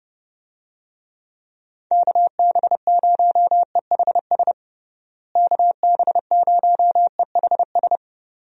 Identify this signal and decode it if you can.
Morse code, heard as a keyed tone, about 30 wpm, 700 Hz. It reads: KB0EHS KB0EHS
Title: USA Call Signs - 6 Characters - Repeated 2x - 30wpm